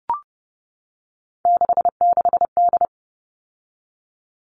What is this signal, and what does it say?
Morse code, heard as a keyed tone, about 30 wpm, 700 Hz. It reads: 66D
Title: All Cut Numbers - 3-Digit 30wpm